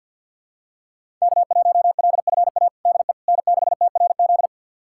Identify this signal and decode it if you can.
Morse code, heard as a keyed tone, about 52 wpm, 700 Hz. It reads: K1LFA DE N6TRZ
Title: USA Callsign DE Callsign - 5 Characters - 50wpm